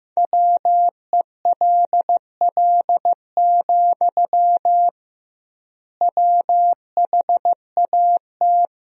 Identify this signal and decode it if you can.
Morse code, heard as a keyed tone, about 15 wpm, 700 Hz. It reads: WELL, WHAT
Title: 2x Word-Spacing - Sentences from Top 200 Words - 15wpm